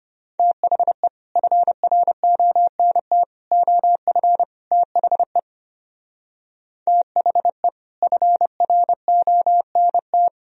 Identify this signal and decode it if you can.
Morse code, heard as a keyed tone, about 28 wpm, 700 Hz. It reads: THE FRONT OF THE THE FRONT
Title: Speed-Racing - Sets of 4 Words 25wpm